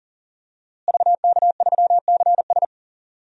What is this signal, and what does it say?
Morse code, heard as a keyed tone, about 39 wpm, 700 Hz. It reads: VK3CS
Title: DXCC-100 Call Signs - 5 Characters - Repeated 2x - 40wpm